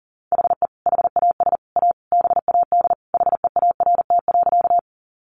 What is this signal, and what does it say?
Morse code, heard as a keyed tone, about 40 wpm, 700 Hz. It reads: HE HAS A BAD HEART.
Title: Sentences from Top 1200 Words 40wpm